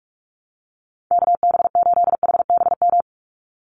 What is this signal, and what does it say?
Morse code, heard as a keyed tone, about 44 wpm, 700 Hz. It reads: KB8HBM
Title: USA Call Signs 45wpm